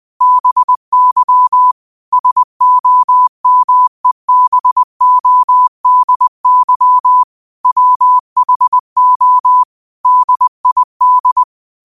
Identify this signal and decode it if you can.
Morse code, heard as a keyed tone, about 20 wpm, 1000 Hz. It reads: BY SOMEBODY WHO DID